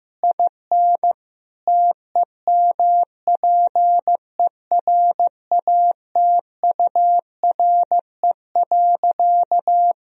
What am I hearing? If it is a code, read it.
Morse code, heard as a keyed tone, about 15 wpm, 700 Hz. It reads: IN TEMPERATURE.
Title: Sentences from Top 1000 Words - Review Part 4 - 15wpm